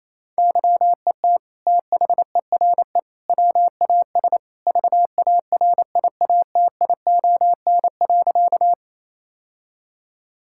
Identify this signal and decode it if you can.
Morse code, heard as a keyed tone, about 28 wpm, 700 Hz. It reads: YET THERE WAS VARIATION.
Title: Sentences from Top 1600 Words 28wpm